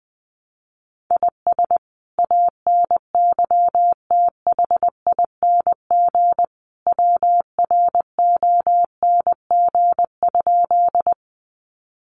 Morse code, heard as a keyed tone, about 20 wpm, 700 Hz. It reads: IS ANYTHING WRONG?
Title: Sentences from Top 1000 Words 20wpm